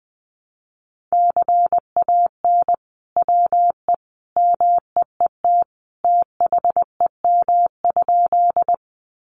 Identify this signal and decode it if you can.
Morse code, heard as a keyed tone, about 19 wpm, 700 Hz. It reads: CAN WE MEET THEM?
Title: Sentences from Top 1100 Words 20wpm